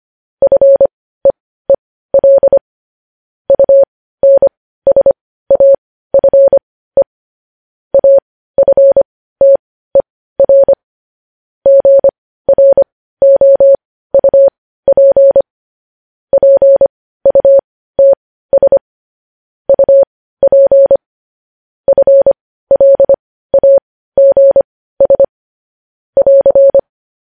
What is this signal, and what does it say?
Morse code, heard as a keyed tone, about 25 wpm, spaced as for 15 wpm, 550 Hz. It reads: FEEL UNSAFE AFTER GROUP PUTS UP FLAGS <AR>
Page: news headlines in morse code at 15 wpm